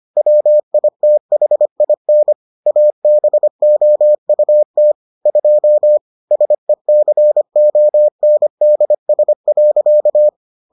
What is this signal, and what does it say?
Morse code, heard as a keyed tone, about 25 wpm, 600 Hz. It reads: WITHIN ABOUT 2 SECONDS.